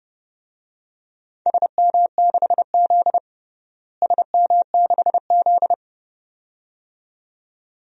Morse code, heard as a keyed tone, about 30 wpm, 700 Hz. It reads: SM6Z SM6Z
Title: DXCC-100 Call Signs - 4 Characters - Repeated 2x - 30wpm